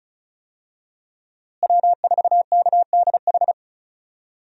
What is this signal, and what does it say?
Morse code, heard as a keyed tone, about 34 wpm, 700 Hz. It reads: W4KDH
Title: USA Call Signs - 5 Characters - Repeated 2x - 35wpm